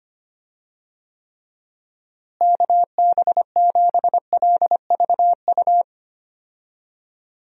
Morse code, heard as a keyed tone, about 25 wpm, 700 Hz. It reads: KB7LVU
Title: USA Call Signs - 6 Characters - Repeated 3x - 25wpm